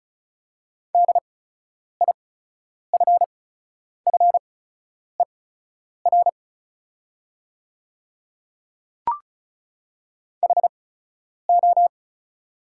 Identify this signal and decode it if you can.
Morse code, heard as a keyed tone, about 36 wpm, spaced as for 10 wpm, 690 Hz. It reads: DIFFER HO
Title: 8x Character Spacing - Top 200 Words - 35wpm